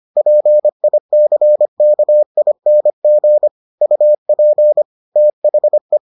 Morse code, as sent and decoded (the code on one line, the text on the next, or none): .--. .. -.-. -.- .. -. --. / ..- .--. / - .... .
PICKING UP THE